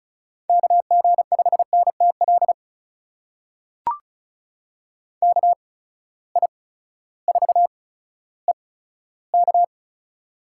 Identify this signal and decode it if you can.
Morse code, heard as a keyed tone, about 36 wpm, spaced as for 16 wpm, 700 Hz. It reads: KG5NTL KI4EK
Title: USA Call Signs - 6 Characters - 8x Character Spacing - 35wpm